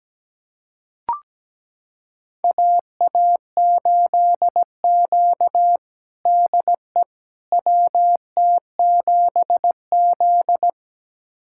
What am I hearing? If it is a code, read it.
Morse code, heard as a keyed tone, about 17 wpm, 700 Hz. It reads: AA8Q DE WT7Z
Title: USA Callsign DE Callsign - 4 Characters - 17wpm